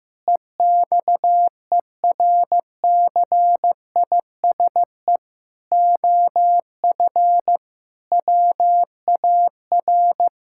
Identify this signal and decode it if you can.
Morse code, heard as a keyed tone, about 15 wpm, 700 Hz. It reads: EXERCISE OF WAR
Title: Sentences from Top 1000 Words - Review Part 6 - 15wpm